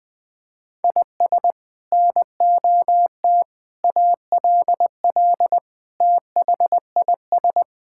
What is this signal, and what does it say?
Morse code, heard as a keyed tone, about 20 wpm, 700 Hz. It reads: IS NOT ALL THIS
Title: Sentences from Top 1600 Words 20wpm